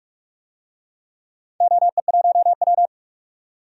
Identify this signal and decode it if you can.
Morse code, heard as a keyed tone, about 44 wpm, 700 Hz. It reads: OE1W
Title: DXCC-100 Call Signs - 4 Characters - 45wpm